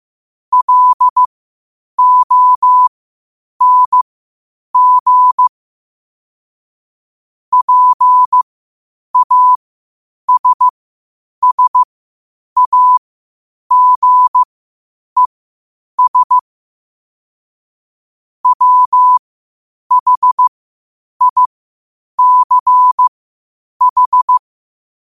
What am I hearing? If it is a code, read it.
Morse code, heard as a keyed tone, about 15 wpm, spaced as for 8 wpm, 1000 Hz. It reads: LONG PASSAGES WHICH
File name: qotdfiles / 8wpm